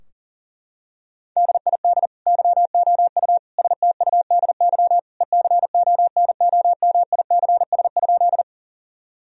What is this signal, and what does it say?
Morse code, heard as a keyed tone, about 40 wpm, 700 Hz. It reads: DID YOU STUDY ECONOMICS?
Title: Sentences from Top 1000 Words - Review Part 3 - 40wpm